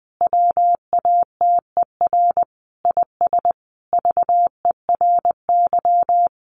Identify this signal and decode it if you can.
Morse code, heard as a keyed tone, about 20 wpm, 700 Hz. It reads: WATER IS VERY
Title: Sentences from Top 1500 Words 20wpm